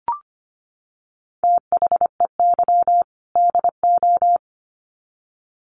Morse code, heard as a keyed tone, about 25 wpm, 700 Hz. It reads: THEY DO
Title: Sets of 2 Words 25wpm